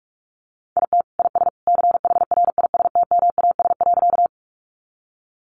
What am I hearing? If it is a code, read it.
Morse code, heard as a keyed tone, about 44 wpm, 700 Hz. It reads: IT IS CHRISTMAS.
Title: Easy Phrases 45wpm